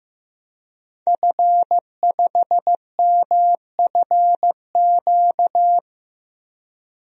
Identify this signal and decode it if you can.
Morse code, heard as a keyed tone, about 15 wpm, 700 Hz. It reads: F5MFQ
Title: DXCC-100 Call Signs - 5 Characters - Repeated 2x - 15wpm